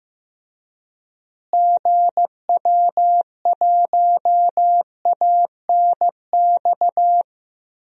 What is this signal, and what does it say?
Morse code, heard as a keyed tone, about 15 wpm, 700 Hz. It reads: GW1ANX
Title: DXCC-100 Call Signs - 6 Characters - Repeated 3x - 15wpm